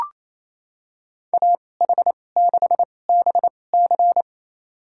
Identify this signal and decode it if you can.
Morse code, heard as a keyed tone, about 28 wpm, spaced as for 20 wpm, 700 Hz. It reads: AH6BC
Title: USA Call Signs - 5 Characters - 2x Character Spacing - 28wpm